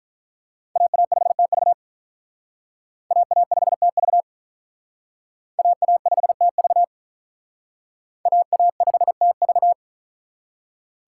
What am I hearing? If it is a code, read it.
Morse code, heard as a keyed tone, about 45 wpm, 700 Hz. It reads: AA5TV AA5TV AA5TV AA5TV